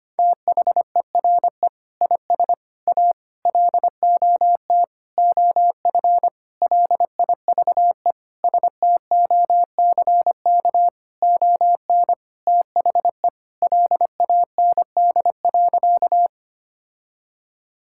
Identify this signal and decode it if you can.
Morse code, heard as a keyed tone, about 25 wpm, 700 Hz. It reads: THERE IS A LOT OF LIVE STOCK ON THE LAND.
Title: Sentences from Top 1000 Words - Review Part 5 - 25wpm